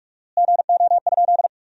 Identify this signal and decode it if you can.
Morse code, heard as a keyed tone, about 46 wpm, 700 Hz. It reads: GO?
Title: 4x Word-Spacing - Sentences from Top 200 Words - 45wpm